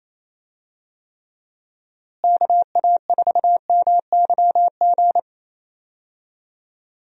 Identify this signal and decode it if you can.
Morse code, heard as a keyed tone, about 28 wpm, 700 Hz. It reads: KA4MYG
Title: USA Call Signs - 6 Characters - Repeated 3x - 28wpm